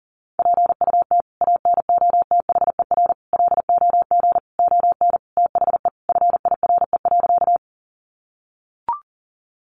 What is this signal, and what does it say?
Morse code, heard as a keyed tone, about 40 wpm, 700 Hz. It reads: PUT ANOTHER LOG ON THE FIRE.
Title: Sentences from Top 1400 Words 40wpm